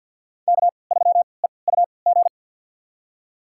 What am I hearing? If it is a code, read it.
Morse code, heard as a keyed tone, about 47 wpm, spaced as for 28 wpm, 700 Hz. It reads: K3EUG
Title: USA Call Signs - 5 Characters - 3x Character Spacing - 50wpm